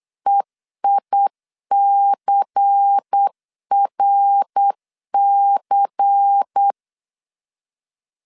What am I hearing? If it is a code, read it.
Morse code, heard as a keyed tone, about 8 wpm, 800 Hz. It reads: EICRC
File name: LEZIONI / 6 PKX0 / 6 PKX0 8wpm 3341